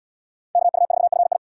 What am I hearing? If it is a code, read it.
Morse code, heard as a keyed tone, about 72 wpm, 700 Hz. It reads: KU4FI